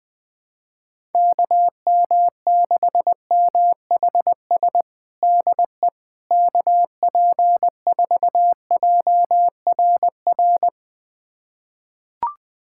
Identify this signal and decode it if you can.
Morse code, heard as a keyed tone, about 21 wpm, 700 Hz. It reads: KM6MHS DE KP4JRR
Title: USA Callsign DE Callsign - 6 Characters - 20wpm